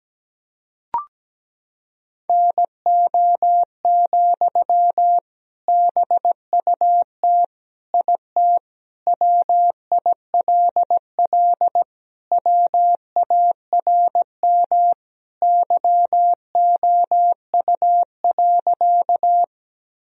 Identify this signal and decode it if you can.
Morse code, heard as a keyed tone, about 17 wpm, 700 Hz. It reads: NO, BUT IT WILL WARM YOU.
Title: Sentences from Top 1000 Words - Review Part 2 - 17wpm